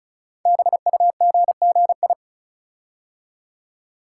Morse code, heard as a keyed tone, about 36 wpm, 690 Hz. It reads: BUGGI
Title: Code Groups - 5 Letter:Numbers - 35wpm